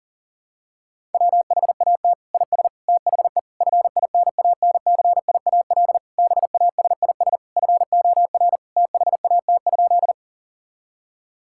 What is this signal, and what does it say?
Morse code, heard as a keyed tone, about 40 wpm, 700 Hz. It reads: WHAT IS THE FINANCIAL BASIS FOR THAT?
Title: Sentences from Top 1000 Words - Review Part 5 - 40wpm